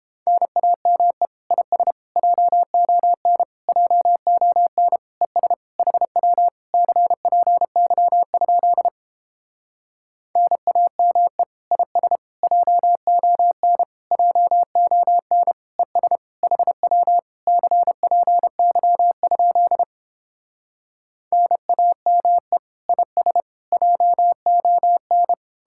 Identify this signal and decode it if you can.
Morse code, heard as a keyed tone, about 31 wpm, 700 Hz. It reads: NAME IS JON JON ES HW CPY? NAME IS JON JON ES HW CPY? NAME IS JON